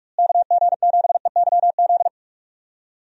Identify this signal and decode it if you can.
Morse code, heard as a keyed tone, about 46 wpm, 700 Hz. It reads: KG7EYZ
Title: USA Call Signs - 6 Characters - Repeated 3x - 45wpm